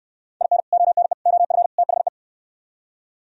Morse code, heard as a keyed tone, about 64 wpm, 700 Hz. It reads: IN ONE OF THE